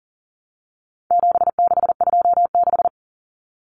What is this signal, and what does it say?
Morse code, heard as a keyed tone, about 40 wpm, 700 Hz. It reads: 7626
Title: Numbers - 4-Digit 40wpm